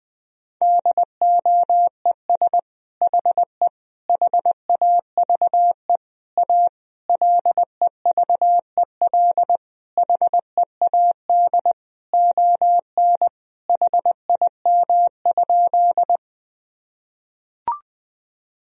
Morse code, heard as a keyed tone, about 20 wpm, 700 Hz. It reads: DOES HE HAVE A LEVEL HEAD ON HIM?
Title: Sentences from Top 1000 Words - Review Part 4 - 20wpm